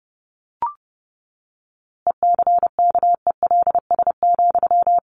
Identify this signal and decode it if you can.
Morse code, heard as a keyed tone, about 30 wpm, 700 Hz. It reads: ECKELS,
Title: A Sound Of Thunder 30wpm